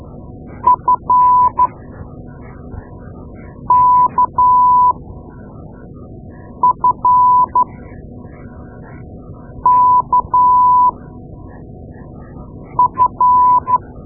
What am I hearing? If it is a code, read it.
Morse code, heard as a keyed tone, about 12 wpm, spaced as for 4 wpm, 1000 Hz. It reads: FKFKF
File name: Morse Lessons / Morse Lessons / Tape-2A